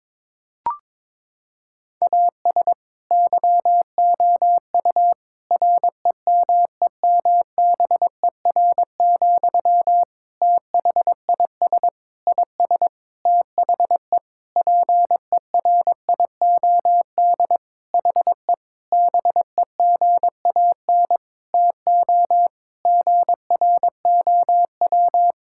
Morse code, heard as a keyed tone, about 22 wpm, 700 Hz. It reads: AS YOU REMEMBER, THIS IS THE PERIOD HE BEGAN TO GROW
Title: Sentences from Top 1000 Words - Review Part 5 - 22wpm